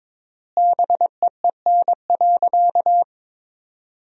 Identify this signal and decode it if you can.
Morse code, heard as a keyed tone, about 22 wpm, 700 Hz. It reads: BEEN.
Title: 3x Word-Spacing - Sentences from Top 300 Words - 22wpm